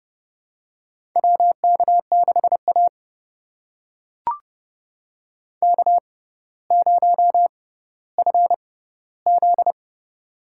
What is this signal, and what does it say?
Morse code, heard as a keyed tone, about 30 wpm, spaced as for 15 wpm, 700 Hz. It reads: WK6A K0FZ
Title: USA Call Signs - 4 Characters - 6x Character Spacing - 30wpm